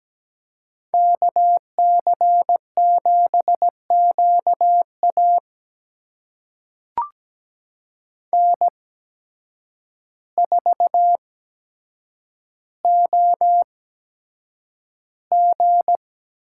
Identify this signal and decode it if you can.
Morse code, heard as a keyed tone, about 17 wpm, spaced as for 8 wpm, 700 Hz. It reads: KC7QA N4OG
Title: USA Call Signs - 5 Characters - 8x Character Spacing - 17wpm